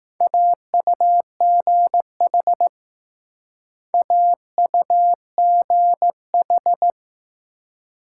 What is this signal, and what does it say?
Morse code, heard as a keyed tone, about 20 wpm, 700 Hz. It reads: AUGH AUGH